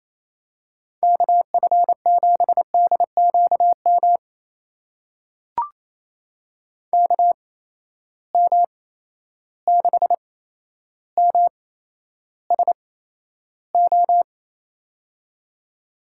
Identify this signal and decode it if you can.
Morse code, heard as a keyed tone, about 28 wpm, spaced as for 12 wpm, 700 Hz. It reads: KF7DQM KM6MSO
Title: USA Call Signs - 6 Characters - 8x Character Spacing - 28wpm